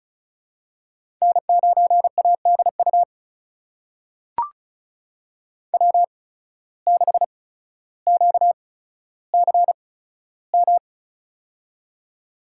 Morse code, heard as a keyed tone, about 36 wpm, spaced as for 15 wpm, 700 Hz. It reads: N9ADU W6QCM
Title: USA Call Signs - 5 Characters - 8x Character Spacing - 35wpm